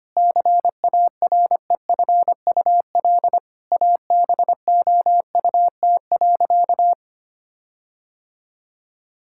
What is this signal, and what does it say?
Morse code, heard as a keyed tone, about 25 wpm, 700 Hz. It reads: CAREFUL ABOUT.
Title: Sentences from Top 1600 Words 25wpm